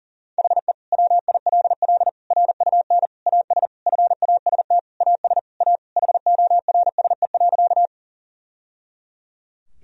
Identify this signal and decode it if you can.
Morse code, heard as a keyed tone, about 40 wpm, 700 Hz. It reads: HE WILL RUN AS FAST AS A HORSE.
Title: Sentences from Top 1000 Words - Review Part 2 - 40wpm